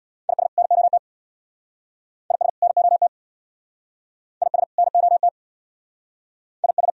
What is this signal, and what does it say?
Morse code, heard as a keyed tone, about 67 wpm, 690 Hz. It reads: IS NOT IS NOT IS NOT IS